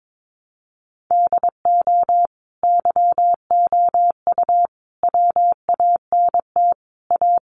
Morse code, heard as a keyed tone, about 22 wpm, 700 Hz. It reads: DO YOU WANT A
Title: Sentences from Top 1300 Words 22wpm